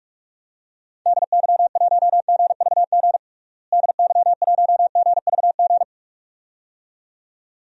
Morse code, heard as a keyed tone, about 46 wpm, 700 Hz. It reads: DY1GVG DY1GVG
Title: DXCC-100 Call Signs - 6 Characters - Repeated 2x - 45wpm